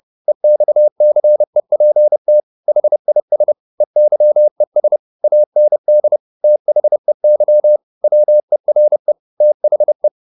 Morse code, heard as a keyed tone, about 30 wpm, 600 Hz. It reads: EXCEPT HIS EYES AND THEY WERE THE
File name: the old man and the sea / the old man and the sea 1 30wpm